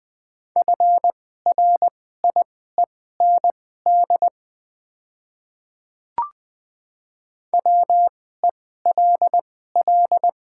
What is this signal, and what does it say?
Morse code, heard as a keyed tone, about 20 wpm, spaced as for 15 wpm, 700 Hz. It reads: FRIEND WELL
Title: 2x Character Spacing - Top 300 Words - 20wpm